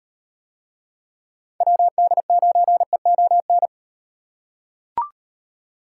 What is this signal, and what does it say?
Morse code, heard as a keyed tone, about 37 wpm, 700 Hz. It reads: WD9EON